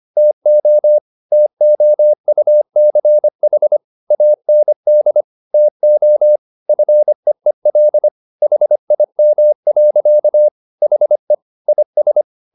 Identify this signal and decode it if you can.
Morse code, heard as a keyed tone, about 25 wpm, 600 Hz. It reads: TO TOUCH AND TO FEEL HIM. HE IS